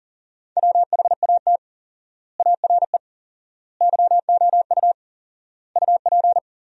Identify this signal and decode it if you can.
Morse code, heard as a keyed tone, about 40 wpm, 700 Hz. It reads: WHAT ARE YOU UP
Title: 4x Word-Spacing - Sentences from Top 100 Words - 40wpm